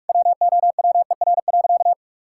Morse code, heard as a keyed tone, about 46 wpm, 700 Hz. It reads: WOWER.
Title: Pangram Challenge v2 - 45wpm